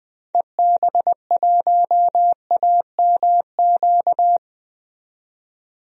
Morse code, heard as a keyed tone, about 20 wpm, 700 Hz. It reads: EB1AMQ
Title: DXCC-100 Call Signs - 6 Characters - 20wpm